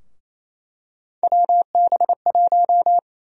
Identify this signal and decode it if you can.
Morse code, heard as a keyed tone, about 28 wpm, 700 Hz. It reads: WB1
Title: USA Call Sign Prefixes - 28wpm